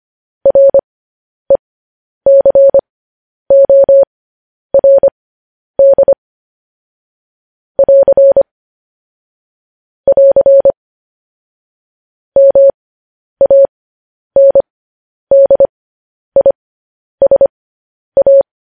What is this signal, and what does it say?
Morse code, heard as a keyed tone, about 25 wpm, spaced as for 10 wpm, 550 Hz. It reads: RECORD <AR> <AR> MANDISA
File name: podcast / News headlines-1x-25x10